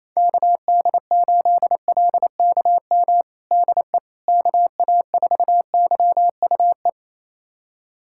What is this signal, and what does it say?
Morse code, heard as a keyed tone, about 28 wpm, 700 Hz. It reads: KD8LKM DE KA4YUE
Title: USA Callsign DE Callsign - 6 Characters - 28wpm